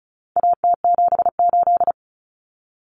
Morse code, heard as a keyed tone, about 34 wpm, 700 Hz. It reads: AT78
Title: Common Cut Numbers - 4-Digit 35wpm